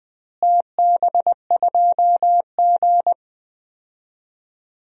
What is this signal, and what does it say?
Morse code, heard as a keyed tone, about 20 wpm, 700 Hz. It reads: TB2G
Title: DXCC-100 Call Signs - 4 Characters - Repeated 3x - 20wpm